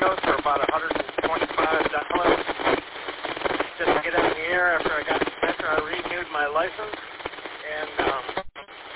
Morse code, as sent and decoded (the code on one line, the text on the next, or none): none